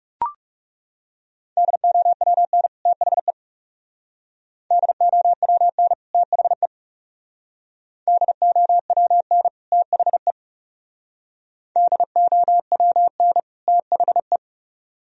Morse code, heard as a keyed tone, about 39 wpm, 700 Hz. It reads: DOWN THE DOWN THE DOWN THE DOWN THE